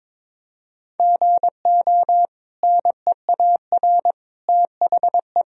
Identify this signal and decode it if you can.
Morse code, heard as a keyed tone, about 22 wpm, 700 Hz. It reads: GO NEAR THE
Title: Sentences from Top 1000 Words - Review Part 1 - 22wpm